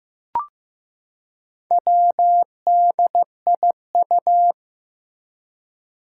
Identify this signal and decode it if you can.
Morse code, heard as a keyed tone, about 15 wpm, 700 Hz. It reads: WDIU
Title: Four Challenging Letter-Number Combinations 15wpm